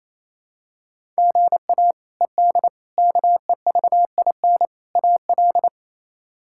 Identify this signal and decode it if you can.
Morse code, heard as a keyed tone, about 28 wpm, 700 Hz. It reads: GA ED KEVIN AL
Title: K1USN SST QSO Practice - 28wpm